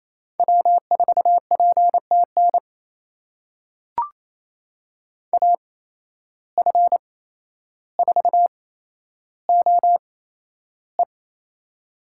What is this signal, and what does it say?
Morse code, heard as a keyed tone, about 28 wpm, spaced as for 12 wpm, 700 Hz. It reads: W4PTN AF4OE
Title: USA Call Signs - 5 Characters - 8x Character Spacing - 28wpm